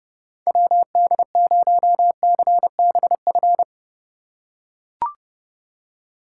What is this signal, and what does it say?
Morse code, heard as a keyed tone, about 31 wpm, 700 Hz. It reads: WD0CBF